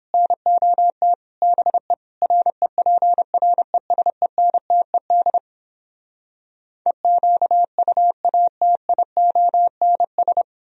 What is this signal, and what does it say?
Morse code, heard as a keyed tone, about 29 wpm, 700 Hz. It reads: NOT BE REPRESENTED EQUATIONS